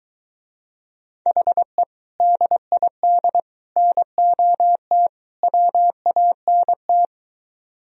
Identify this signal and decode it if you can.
Morse code, heard as a keyed tone, about 22 wpm, 700 Hz. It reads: HE DID NOT WANT